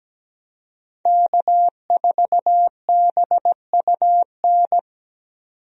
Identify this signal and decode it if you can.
Morse code, heard as a keyed tone, about 17 wpm, 700 Hz. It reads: K4BUN
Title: USA Call Signs - 5 Characters - 17wpm